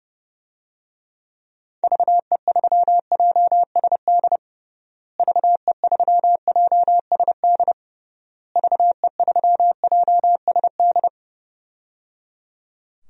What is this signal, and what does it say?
Morse code, heard as a keyed tone, about 29 wpm, 700 Hz. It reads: VE3JSD VE3JSD VE3JSD
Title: DXCC-100 Call Signs - 6 Characters - Repeated 3x - 30wpm